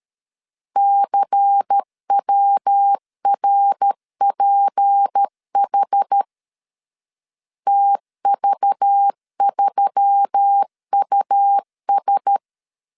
Morse code, heard as a keyed tone, about 13 wpm, 800 Hz. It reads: CWRPH TV3US
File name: LEZIONI / 6 PKX0 / 6 PK 12wpm 2006